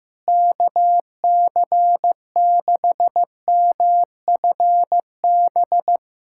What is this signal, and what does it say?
Morse code, heard as a keyed tone, about 16 wpm, 700 Hz. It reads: KC6MFB
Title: USA Callsign DE Callsign - 6 Characters - 15wpm